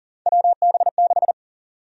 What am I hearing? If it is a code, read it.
Morse code, heard as a keyed tone, about 40 wpm, 700 Hz. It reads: WB6
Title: USA Call Sign Prefixes - 40wpm